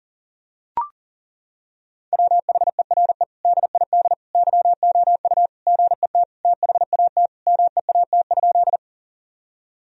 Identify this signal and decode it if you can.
Morse code, heard as a keyed tone, about 40 wpm, 700 Hz. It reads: WHERE DID YOU GET THAT MEAT?
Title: Sentences from Top 1000 Words - Review Part 3 - 40wpm